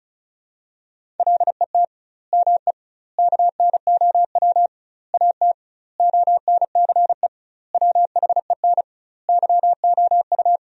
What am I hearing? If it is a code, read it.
Morse code, heard as a keyed tone, about 34 wpm, 700 Hz. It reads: LET ME KNOW AT ONCE WHEN YOU
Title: 2x Word-Spacing - Sentences from Top 300 Words - 35wpm